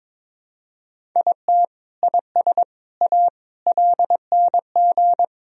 Morse code, heard as a keyed tone, about 22 wpm, 700 Hz. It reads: IT IS A LNG
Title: Sentences from Top 300 Words and Abbreviations 22wpm